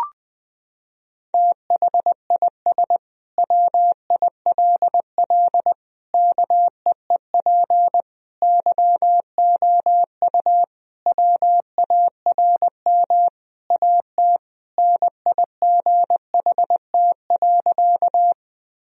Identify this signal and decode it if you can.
Morse code, heard as a keyed tone, about 21 wpm, 700 Hz. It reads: THIS WILL KEEP YOU WARM AT NIGHT.
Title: Sentences from Top 1000 Words - Review Part 2 - 20wpm